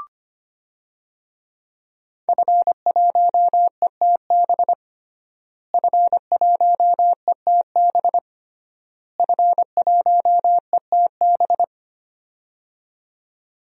Morse code, heard as a keyed tone, about 25 wpm, 700 Hz. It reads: F1ETB F1ETB F1ETB
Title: DXCC-100 Call Signs - 5 Characters - Repeated 3x - 25wpm